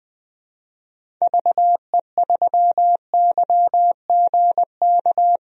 Morse code, heard as a keyed tone, about 21 wpm, 700 Hz. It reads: VE3YGK
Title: DXCC-100 Call Signs - 6 Characters - Repeated 3x - 20wpm